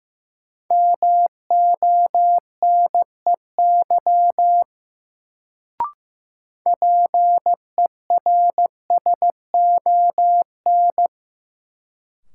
Morse code, heard as a keyed tone, about 15 wpm, 700 Hz. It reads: MONEY PERSON
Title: Sentences from Top 400 Words - New Word Drill - 15wpm